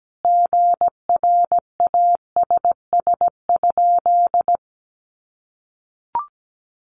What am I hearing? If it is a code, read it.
Morse code, heard as a keyed tone, about 17 wpm, 700 Hz. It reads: GRASS?
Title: A Sound Of Thunder 17wpm